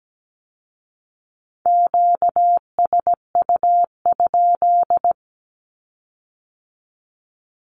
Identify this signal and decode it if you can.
Morse code, heard as a keyed tone, about 17 wpm, 700 Hz. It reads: QSU?
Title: Q-Codes with Spoken Meaning - 17wpm